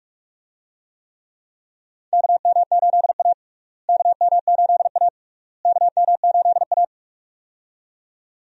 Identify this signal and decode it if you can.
Morse code, heard as a keyed tone, about 46 wpm, 700 Hz. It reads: KM8A KM8A KM8A
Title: USA Call Signs - 4 Characters - Repeated 3x - 45wpm